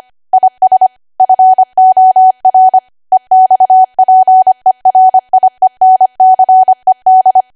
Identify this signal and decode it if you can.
Morse code, heard as a keyed tone, about 25 wpm, 750 Hz. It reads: IS FOR EXPERIENCED